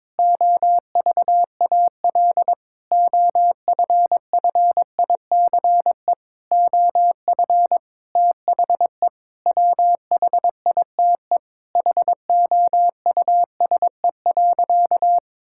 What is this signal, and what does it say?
Morse code, heard as a keyed tone, about 22 wpm, 700 Hz. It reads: OVAL OFFICE OF THE WHITE HOUSE.
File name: practice / 22wpm / Sentences with 8-Word Phrases - 22wpm